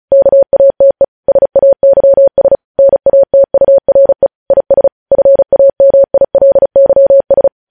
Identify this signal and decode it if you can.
Morse code, heard as a keyed tone, about 36 wpm, 550 Hz. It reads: KATE SAYS NATURE IS FAMILYS
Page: News headlines in Morse code at 35 wpm